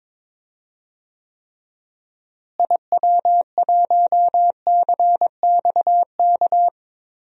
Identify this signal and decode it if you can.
Morse code, heard as a keyed tone, about 22 wpm, 700 Hz. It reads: IW1CXK
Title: DXCC-100 Call Signs - 6 Characters - Repeated 2x - 22wpm